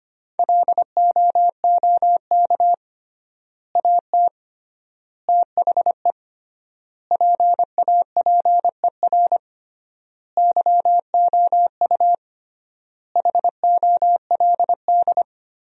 Morse code, heard as a keyed tone, about 25 wpm, 700 Hz. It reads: LOOK AT THE PAPER YOU HOLD
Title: 3x Word-Spacing - Sentences from Top 400 Words - 25wpm